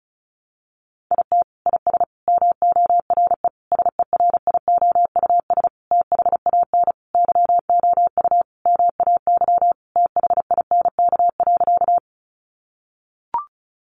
Morse code, heard as a keyed tone, about 34 wpm, 700 Hz. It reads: IT IS MORE SERIOUS THAN YOU MAY THINK.
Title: Sentences from Top 1100 Words 35wpm